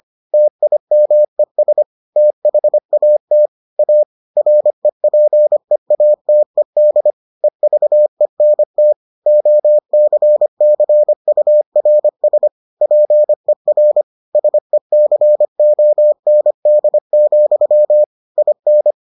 Morse code, heard as a keyed tone, about 25 wpm, 600 Hz. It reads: TIMES THAT A REPEATED EVENT OCCURS PER SECOND, IN